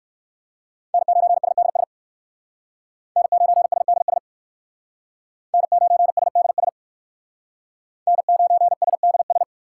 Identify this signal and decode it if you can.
Morse code, heard as a keyed tone, about 60 wpm, 700 Hz. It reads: N9SDS N9SDS N9SDS N9SDS